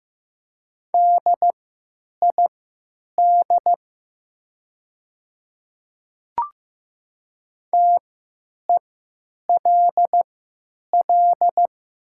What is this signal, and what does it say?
Morse code, heard as a keyed tone, about 15 wpm, spaced as for 9 wpm, 700 Hz. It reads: DID TELL
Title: 3x Character Spacing - Top 200 Words - 15wpm